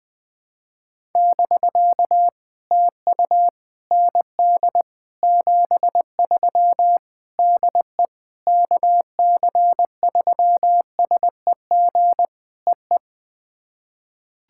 Morse code, heard as a keyed tone, about 20 wpm, 700 Hz. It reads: <BK> TU ND 73 DE KC3SEG EE
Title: POTA QSO Practice - 20wpm